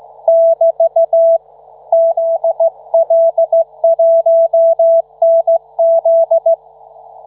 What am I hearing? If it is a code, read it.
Morse code, heard as a keyed tone, about 16 wpm, 660 Hz. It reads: <BT> ZL1NZ